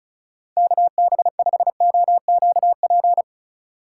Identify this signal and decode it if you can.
Morse code, heard as a keyed tone, about 36 wpm, 700 Hz. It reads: KB5OQP
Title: USA Call Signs - 6 Characters - 3x Character Spacing - 35wpm